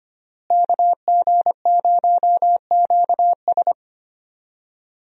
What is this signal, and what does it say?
Morse code, heard as a keyed tone, about 25 wpm, 700 Hz. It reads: KG0QS
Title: USA Call Signs - 5 Characters - Repeated 2x - 25wpm